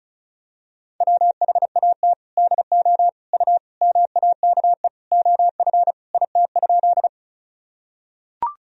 Morse code, heard as a keyed tone, about 36 wpm, 700 Hz. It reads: WHAT DO U MAKE OF IT?
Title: Sentences from Top 100 Words and Abbreviations 35wpm